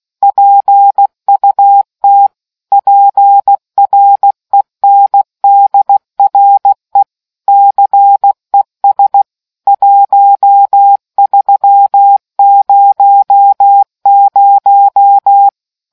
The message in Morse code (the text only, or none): PUT PRENDRE CES 1300